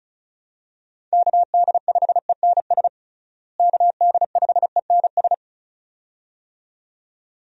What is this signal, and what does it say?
Morse code, heard as a keyed tone, about 36 wpm, 700 Hz. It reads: KD5ENS KD5ENS
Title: USA Call Signs - 6 Characters - Repeated 2x - 35wpm